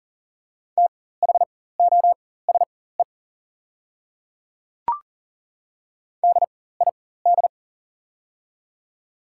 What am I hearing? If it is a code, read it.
Morse code, heard as a keyed tone, about 40 wpm, spaced as for 19 wpm, 700 Hz. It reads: THOSE DID
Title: 4x Character Spacing - Top 300 Words - 40wpm